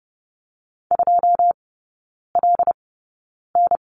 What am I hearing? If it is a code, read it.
Morse code, heard as a keyed tone, about 30 wpm, spaced as for 9 wpm, 700 Hz. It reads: 2LN
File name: 081 Lesson 14e - Number 2 - Three Characters 30wpm